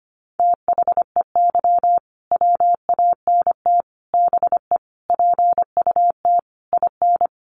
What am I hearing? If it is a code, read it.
Morse code, heard as a keyed tone, about 25 wpm, 700 Hz. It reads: THEY WANT BE PUT IN
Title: Sentences from Top 1000 Words 25wpm